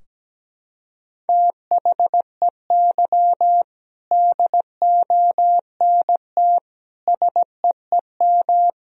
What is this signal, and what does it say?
Morse code, heard as a keyed tone, about 17 wpm, 700 Hz. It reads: THEY DONT SEEM
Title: Sentences from Top 1000 Words - Review Part 2 - 17wpm